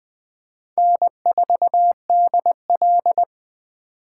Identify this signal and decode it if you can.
Morse code, heard as a keyed tone, about 23 wpm, 700 Hz. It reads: N4DL
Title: Speed Racing - USA 1x2 Call Signs 20wpm